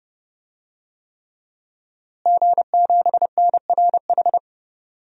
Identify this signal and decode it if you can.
Morse code, heard as a keyed tone, about 32 wpm, 700 Hz. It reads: G7NRH
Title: DXCC-100 Call Signs - 5 Characters - Repeated 3x - 30wpm